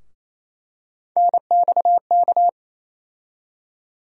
Morse code, heard as a keyed tone, about 28 wpm, 700 Hz. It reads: NXK
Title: USA Call Sign Postfixes - 3 Letters 28wpm